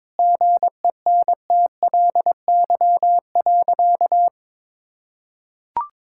Morse code, heard as a keyed tone, about 22 wpm, 700 Hz. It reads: GENTLY.
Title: Sentences from Top 1600 Words 22wpm